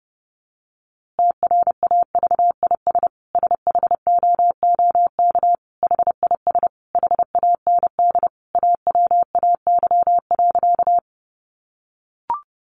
Morse code, heard as a keyed tone, about 30 wpm, 700 Hz. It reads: TRAVIS SHOOK HIS HAND AWAY.
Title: A Sound Of Thunder 30wpm